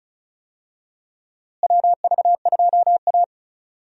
Morse code, heard as a keyed tone, about 36 wpm, 700 Hz. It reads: WV2A
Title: USA Call Signs - 4 Characters - Repeated 2x - 35wpm